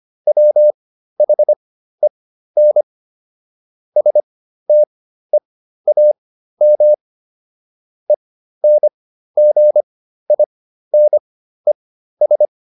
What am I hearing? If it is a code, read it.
Morse code, heard as a keyed tone, about 25 wpm, spaced as for 13 wpm, 600 Hz. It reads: WHEN STEAM ENGINES